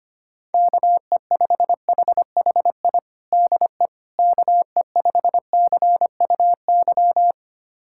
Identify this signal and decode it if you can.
Morse code, heard as a keyed tone, about 25 wpm, 700 Hz. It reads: KE5HHI DE KE5CUY
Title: USA Callsign DE Callsign - 6 Characters - 25wpm